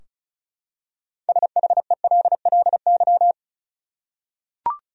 Morse code, heard as a keyed tone, about 34 wpm, 700 Hz. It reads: SHELLY
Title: USA Common Names - 35wpm